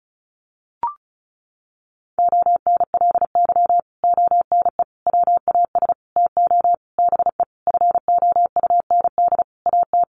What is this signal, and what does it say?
Morse code, heard as a keyed tone, about 34 wpm, 700 Hz. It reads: ONLY ONE WAS TO BE FOUND AT
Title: Sentences from Top 1400 Words 35wpm